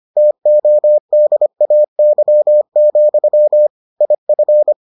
Morse code, heard as a keyed tone, about 25 wpm, 600 Hz. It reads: TODAY, IF